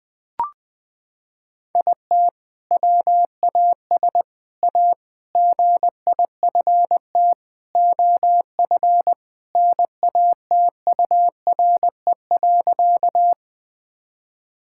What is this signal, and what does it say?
Morse code, heard as a keyed tone, about 20 wpm, 700 Hz. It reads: IT WAS A GIFT OF NATURE.
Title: Sentences from Top 1000 Words - Review Part 6 - 20wpm